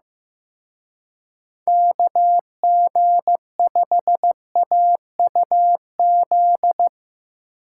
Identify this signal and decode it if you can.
Morse code, heard as a keyed tone, about 17 wpm, 700 Hz. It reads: KG5AUZ
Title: Speed Racing - USA 2x3 Call Signs 15wpm